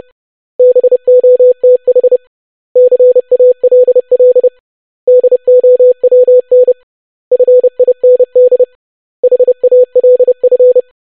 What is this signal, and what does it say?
Morse code, heard as a keyed tone, about 30 wpm, 500 Hz. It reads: BOTH CALL DOWN FIND HALF
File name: basic500-25wpm